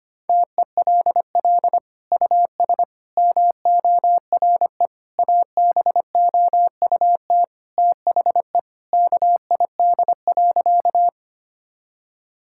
Morse code, heard as a keyed tone, about 25 wpm, 700 Hz. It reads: TELL US MORE ABOUT THE KID.
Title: Sentences from Top 1600 Words 25wpm